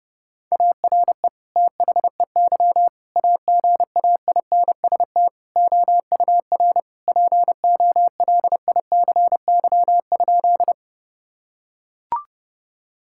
Morse code, heard as a keyed tone, about 30 wpm, 700 Hz. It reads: ARE THEY AGAINST OUR POLICY?
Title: Sentences from Top 1000 Words - Review Part 5 - 30wpm